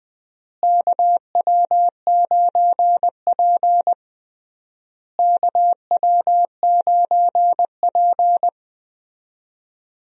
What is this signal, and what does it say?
Morse code, heard as a keyed tone, about 20 wpm, 700 Hz. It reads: KW9P KW9P
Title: USA Call Signs - 4 Characters - Repeated 3x - 20wpm